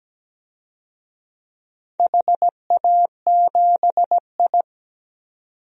Morse code, heard as a keyed tone, about 17 wpm, 700 Hz. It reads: HA7I
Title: DXCC-100 Call Signs - 4 Characters - 17wpm